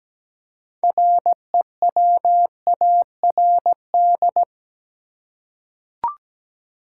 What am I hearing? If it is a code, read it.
Morse code, heard as a keyed tone, about 17 wpm, 700 Hz. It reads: REWARD
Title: Six Letter Words - 17wpm